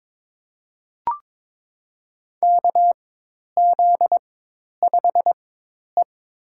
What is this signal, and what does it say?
Morse code, heard as a keyed tone, about 22 wpm, spaced as for 10 wpm, 700 Hz. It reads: KZ5E
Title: USA Call Signs - 4 Characters - 4x Character Spacing - 22wpm